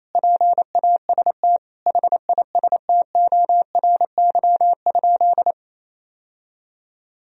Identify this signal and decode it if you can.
Morse code, heard as a keyed tone, about 27 wpm, 700 Hz. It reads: PAST HISTORY?
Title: Sentences from Top 1000 Words - Review Part 3 - 28wpm